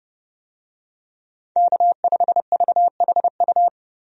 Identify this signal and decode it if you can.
Morse code, heard as a keyed tone, about 31 wpm, 700 Hz. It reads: K5VHU
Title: USA Call Signs - 5 Characters - 30wpm